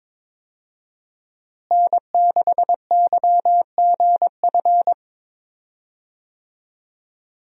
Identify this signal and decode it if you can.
Morse code, heard as a keyed tone, about 22 wpm, 700 Hz. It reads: N6YGF
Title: USA Call Signs - 5 Characters - Repeated 3x - 22wpm